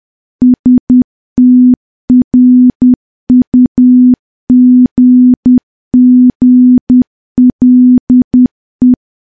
Morse code, heard as a keyed tone, about 10 wpm, 250 Hz. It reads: STRUGGLE
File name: morse / morse-training-10WPM